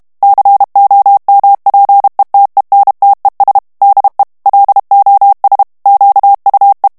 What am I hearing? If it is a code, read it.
Morse code, heard as a keyed tone, about 32 wpm, 800 Hz. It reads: COMPETENTES DE LOS QUE
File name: ESPAGNOL TEXTE EN CLAIR 2 VITESSE 30 MOTS MINUTES